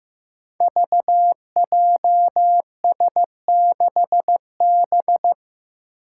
Morse code, heard as a keyed tone, about 15 wpm, 700 Hz. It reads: VJS6B
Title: Five Challenging Letter-Number Combinations 15wpm